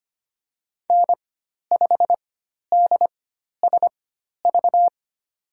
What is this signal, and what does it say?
Morse code, heard as a keyed tone, about 25 wpm, spaced as for 12 wpm, 700 Hz. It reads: N5DSV
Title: USA Call Signs - 5 Characters - 4x Character Spacing - 25wpm